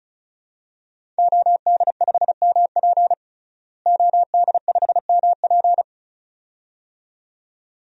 Morse code, heard as a keyed tone, about 34 wpm, 700 Hz. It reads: OD5MP OD5MP
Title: DXCC-100 Call Signs - 5 Characters - Repeated 2x - 35wpm